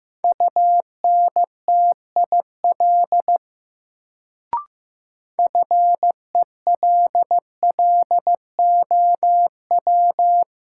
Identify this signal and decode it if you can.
Morse code, heard as a keyed tone, about 15 wpm, 700 Hz. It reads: UNTIL FELLOW
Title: Sets of 5 Words Encore - New Word Drill - 15wpm